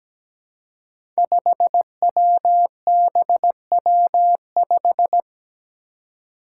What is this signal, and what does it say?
Morse code, heard as a keyed tone, about 17 wpm, 700 Hz. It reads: 5WBW5
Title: Five Challenging Letter-Number Combinations 17wpm